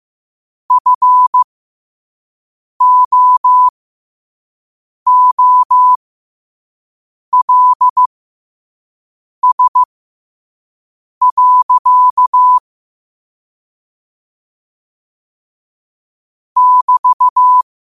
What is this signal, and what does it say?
Morse code, heard as a keyed tone, about 15 wpm, spaced as for 5 wpm, 1000 Hz. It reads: FOOLS. <BT>
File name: qotdfiles / 5wpm